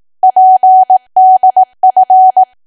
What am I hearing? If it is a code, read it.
Morse code, heard as a keyed tone, about 18 wpm, 750 Hz. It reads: PDF
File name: cwa / W1AW-Level2 / 18WPM